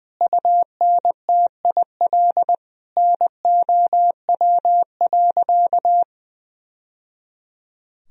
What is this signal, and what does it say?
Morse code, heard as a keyed tone, about 20 wpm, 700 Hz. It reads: UNTIL NOW.
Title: Sentences from Top 1000 Words - Review Part 1 - 20wpm